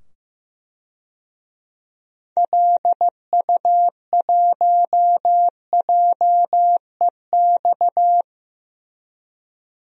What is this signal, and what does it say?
Morse code, heard as a keyed tone, about 15 wpm, 700 Hz. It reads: LU1JEX
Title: DXCC-100 Call Signs - 6 Characters - 15wpm